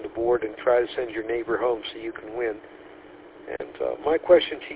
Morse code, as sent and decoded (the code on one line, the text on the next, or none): none